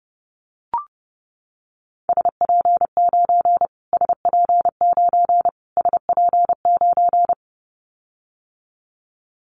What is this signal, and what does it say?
Morse code, heard as a keyed tone, about 30 wpm, 700 Hz. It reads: SP9 SP9 SP9
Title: Callsign Prefixes for Top 25 DXCC - Repeated 3x - 30wpm